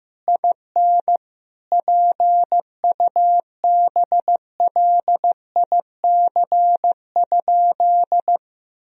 Morse code, heard as a keyed tone, about 15 wpm, 700 Hz. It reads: IN PUBLIC?
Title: Sentences from Top 1000 Words - Review Part 3 - 15wpm